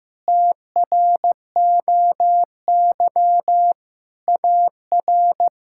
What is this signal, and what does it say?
Morse code, heard as a keyed tone, about 15 wpm, 700 Hz. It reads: TROY AR
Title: K1USN SST QSO Practice - 15wpm